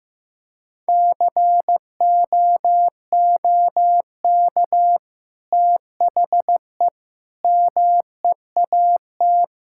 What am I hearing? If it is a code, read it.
Morse code, heard as a keyed tone, about 16 wpm, 700 Hz. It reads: COOK THE MEAT
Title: Sentences from Top 1000 Words - Review Part 6 - 15wpm